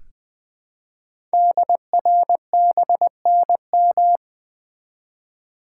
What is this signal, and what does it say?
Morse code, heard as a keyed tone, about 20 wpm, 700 Hz. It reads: DRBNM
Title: Code Groups - 5 Letters - 20wpm